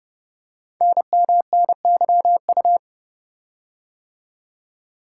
Morse code, heard as a keyed tone, about 30 wpm, 700 Hz. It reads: NMNYU
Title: Code Groups - 5 Letters - 30wpm